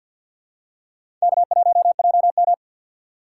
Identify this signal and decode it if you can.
Morse code, heard as a keyed tone, about 50 wpm, 700 Hz. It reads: K1JM
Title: USA Call Signs - 4 Characters - 3x Character Spacing - 50wpm